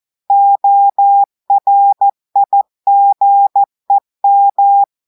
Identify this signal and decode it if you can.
Morse code, heard as a keyed tone, about 14 wpm, 800 Hz. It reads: ORIGEM
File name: audios / portugais / portugais texte en clair 1 vitesse 14 mots minutes